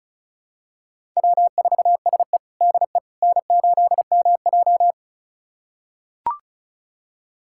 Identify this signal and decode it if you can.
Morse code, heard as a keyed tone, about 36 wpm, 700 Hz. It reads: W4SE DE N8MJ
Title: USA Callsign DE Callsign - 4 Characters - 35wpm